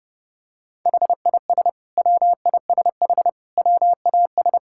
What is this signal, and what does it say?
Morse code, heard as a keyed tone, about 30 wpm, 700 Hz. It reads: HIS WISH WAS
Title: Sentences from Top 1600 Words 30wpm